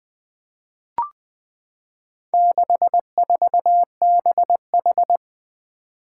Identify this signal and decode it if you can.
Morse code, heard as a keyed tone, about 20 wpm, 700 Hz. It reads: 64BH
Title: Four Letter-Number Combinations - 20wpm